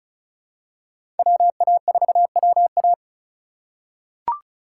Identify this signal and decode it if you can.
Morse code, heard as a keyed tone, about 36 wpm, 700 Hz. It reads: WA4WA
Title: USA Call Signs - 4x Character Spacing - 35wpm